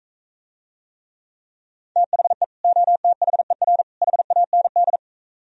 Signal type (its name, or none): Morse code